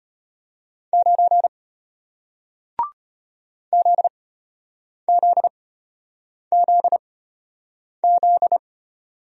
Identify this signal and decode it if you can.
Morse code, heard as a keyed tone, about 33 wpm, spaced as for 8 wpm, 700 Hz. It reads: 9 ZZZZ